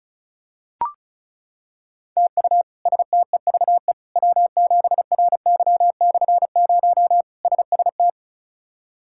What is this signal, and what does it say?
Morse code, heard as a keyed tone, about 36 wpm, 700 Hz. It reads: TU STEVE W7RY/0 SST
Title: K1USN SST QSO Practice - 35wpm